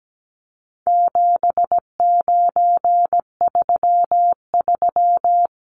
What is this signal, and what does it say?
Morse code, heard as a keyed tone, about 17 wpm, 700 Hz. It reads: 7933
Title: Numbers - 4-Digit 17wpm